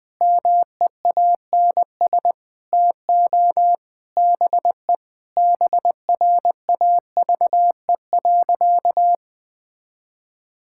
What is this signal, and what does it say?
Morse code, heard as a keyed tone, about 20 wpm, 700 Hz. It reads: MEANS TO BE BRAVE.
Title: Sentences with 5-Word Phrases - Spoken Beginning - 20wpm